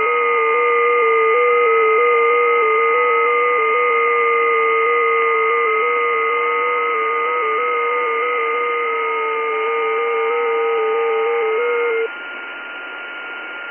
FT8.
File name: test / websdr test14 12k